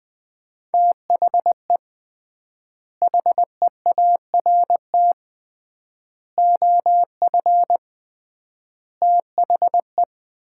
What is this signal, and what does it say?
Morse code, heard as a keyed tone, about 20 wpm, 700 Hz. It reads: THE HEART OF THE
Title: 3x Word-Spacing - Sets of 4 Words - 20wpm